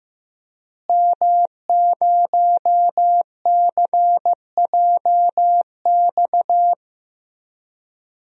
Morse code, heard as a keyed tone, about 15 wpm, 700 Hz. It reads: M0CJX
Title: DXCC-100 Call Signs - 5 Characters - Repeated 3x - 15wpm